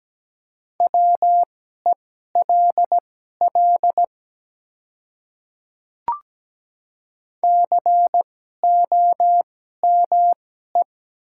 Morse code, heard as a keyed tone, about 17 wpm, spaced as for 12 wpm, 700 Hz. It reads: WELL COME
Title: 2x Character Spacing - Top 300 Words - 17wpm